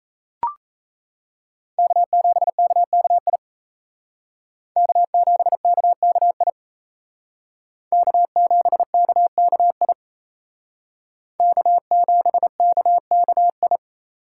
Morse code, heard as a keyed tone, about 37 wpm, 700 Hz. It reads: K7KKI K7KKI K7KKI K7KKI